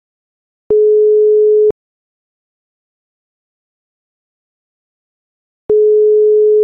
Ringback tone (Europe, 425 Hz).
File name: European ETSI ringback tone